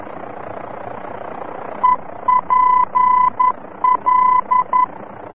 Morse code, heard as a keyed tone, about 11 wpm, 1000 Hz. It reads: EPL
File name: france / mx / EPL-VOR-F